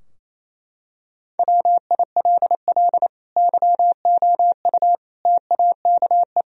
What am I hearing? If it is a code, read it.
Morse code, heard as a keyed tone, about 27 wpm, 700 Hz. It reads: WILL YOU TAKE
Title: Sentences from Top 1000 Words - Review Part 3 - 28wpm